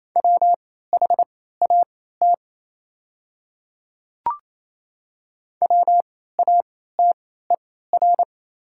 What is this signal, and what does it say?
Morse code, heard as a keyed tone, about 28 wpm, spaced as for 16 wpm, 700 Hz. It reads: WHAT WATER
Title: 3x Character Spacing - Top 200 Words - 28wpm